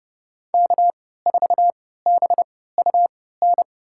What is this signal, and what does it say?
Morse code, heard as a keyed tone, about 30 wpm, spaced as for 17 wpm, 700 Hz. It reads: K4BUN
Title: USA Call Signs - 5 Characters - 3x Character Spacing - 30wpm